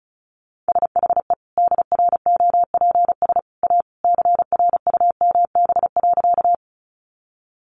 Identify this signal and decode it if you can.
Morse code, heard as a keyed tone, about 34 wpm, 700 Hz. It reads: SHE DROPS A CRUMB.
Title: Easy Phrases 35wpm